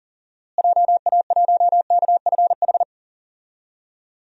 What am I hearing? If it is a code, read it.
Morse code, heard as a keyed tone, about 40 wpm, 700 Hz. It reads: JA1KFH
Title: DXCC-100 Call Signs - 6 Characters - Repeated 3x - 40wpm